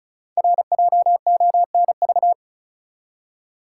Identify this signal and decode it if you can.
Morse code, heard as a keyed tone, about 36 wpm, 700 Hz. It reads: RJONV
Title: Code Groups - 5 Letters - 35wpm